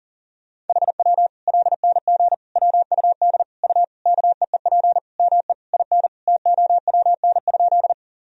Morse code, heard as a keyed tone, about 40 wpm, 700 Hz. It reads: HW LNG WUD U KEEP ME IN TOWN?
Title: Sentences from Top 500 Words and Abbreviations 40wpm